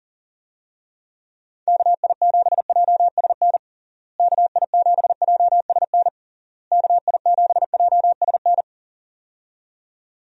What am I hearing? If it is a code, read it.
Morse code, heard as a keyed tone, about 40 wpm, 700 Hz. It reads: KI7JSN KI7JSN KI7JSN
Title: USA Call Signs - 6 Characters - Repeated 3x - 40wpm